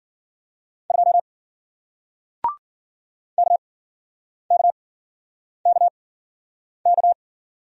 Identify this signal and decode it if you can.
Morse code, heard as a keyed tone, about 53 wpm, spaced as for 9 wpm, 700 Hz. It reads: 2 KKKK